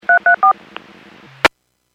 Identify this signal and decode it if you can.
DTMF, heard as telephone touch tones, tones of about 110 ms. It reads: 334